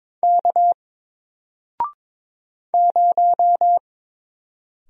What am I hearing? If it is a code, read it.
Morse code, heard as a keyed tone, about 22 wpm, 700 Hz. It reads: K 0
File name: Single-Letter-Number-22wpm